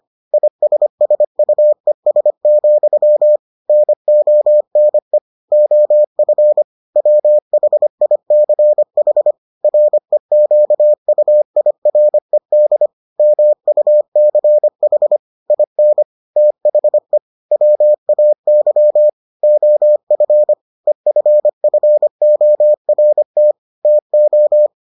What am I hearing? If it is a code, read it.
Morse code, heard as a keyed tone, about 25 wpm, 600 Hz. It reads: ISSUES, NONE OF WHICH REQUIRED MUCH IN THE WAY OF EFFORT TO